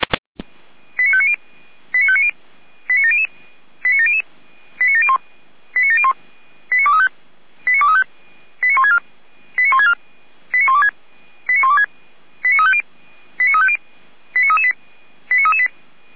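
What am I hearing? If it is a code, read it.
Five-tone selective call (ZVEI), heard as quick runs of five tones, tones of about 70 ms. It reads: 89590 89590 89700 89700 89701 89701 89235 89164 89164 89126 89126 89350 89350 89338 89338
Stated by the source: ZVEI five-tone selective call received on SDR.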